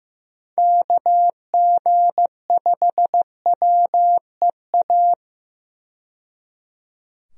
Morse code, heard as a keyed tone, about 15 wpm, 700 Hz. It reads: KG5WEA
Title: USA Call Signs - 6 Characters - 15wpm